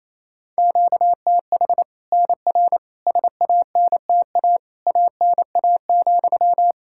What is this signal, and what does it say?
Morse code, heard as a keyed tone, about 28 wpm, 700 Hz. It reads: QTH NR SANTA ANA,
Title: Practice QSO - Exchange 1 - Beginning - 28wpm